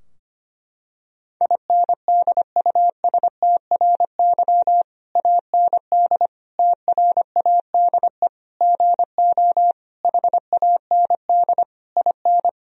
Morse code, heard as a keyed tone, about 25 wpm, 700 Hz. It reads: INDUSTRY AND TRADE GO HAND IN
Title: Sentences from Top 1000 Words - Review Part 4 - 25wpm